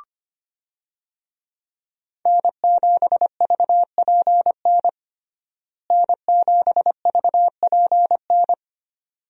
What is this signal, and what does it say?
Morse code, heard as a keyed tone, about 25 wpm, 700 Hz. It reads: N7VPN N7VPN
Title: USA Call Signs - 5 Characters - Repeated 2x - 25wpm